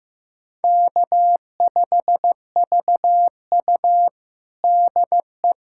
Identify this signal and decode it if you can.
Morse code, heard as a keyed tone, about 15 wpm, 700 Hz. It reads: K5VU DE
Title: USA Callsign DE Callsign - 4 Characters - 15wpm